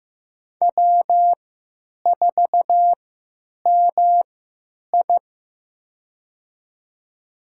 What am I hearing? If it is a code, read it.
Morse code, heard as a keyed tone, about 15 wpm, spaced as for 9 wpm, 700 Hz. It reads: W4MI
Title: USA Call Signs - 3x Character Spacing - 15wpm